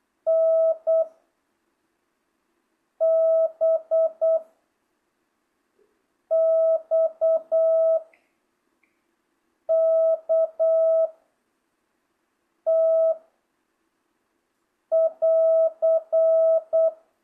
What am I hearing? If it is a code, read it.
Morse code, heard as a keyed tone, about 8 wpm, 660 Hz. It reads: N B X K T <AR>